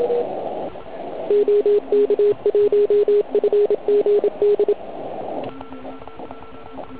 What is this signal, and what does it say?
Morse code, heard as a keyed tone, about 27 wpm, 410 Hz, with radio noise. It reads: OK1FGD
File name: Archiv / CUC129 / CUC129 pileup